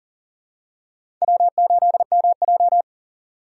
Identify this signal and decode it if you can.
Morse code, heard as a keyed tone, about 40 wpm, 700 Hz. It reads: W8MJ
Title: USA Call Signs - 4 Characters - 4x Character Spacing - 40wpm